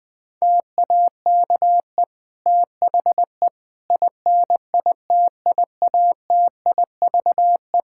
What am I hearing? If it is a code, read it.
Morse code, heard as a keyed tone, about 21 wpm, 700 Hz. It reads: TAKE THE INITIATIVE
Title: Difficult Sentences with Hints - 20wpm